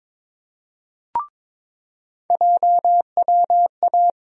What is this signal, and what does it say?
Morse code, heard as a keyed tone, about 22 wpm, 700 Hz. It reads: JWA
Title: USA Call Sign Postfixes - 3 Letters 22wpm